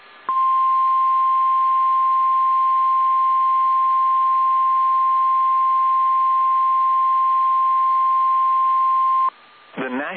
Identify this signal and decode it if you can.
Emergency Alert System (SAME) tones (the 1050 Hz attention tone). It is heard from 0.3 to 9.3 s.